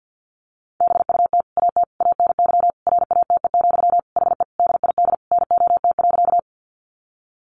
Morse code, heard as a keyed tone, about 50 wpm, 700 Hz. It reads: BUT AT ANY RATE, HE DID NOT.
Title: Sentences from Top 700 Words 50wpm